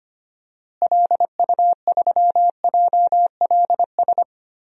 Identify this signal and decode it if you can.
Morse code, heard as a keyed tone, about 25 wpm, 700 Hz. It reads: LU3JLS
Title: DXCC-100 Call Signs - 6 Characters - Repeated 3x - 25wpm